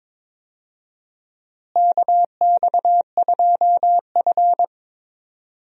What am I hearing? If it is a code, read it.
Morse code, heard as a keyed tone, about 22 wpm, 700 Hz. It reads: KX2F
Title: USA Call Signs - 4 Characters - Repeated 2x - 22wpm